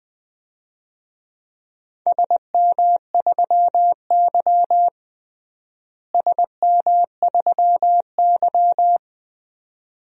Morse code, heard as a keyed tone, about 20 wpm, 700 Hz. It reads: SM3Y SM3Y
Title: DXCC-100 Call Signs - 4 Characters - Repeated 2x - 20wpm